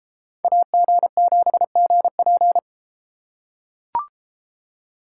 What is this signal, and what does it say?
Morse code, heard as a keyed tone, about 32 wpm, 700 Hz. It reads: AG7GP